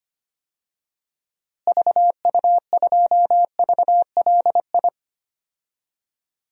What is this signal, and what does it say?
Morse code, heard as a keyed tone, about 25 wpm, 700 Hz. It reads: VU2VLI
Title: DXCC-100 Call Signs - 6 Characters - Repeated 2x - 25wpm